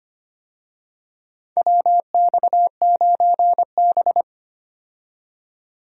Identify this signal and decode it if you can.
Morse code, heard as a keyed tone, about 25 wpm, 700 Hz. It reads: WX9B
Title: USA Call Signs - 4 Characters - 25wpm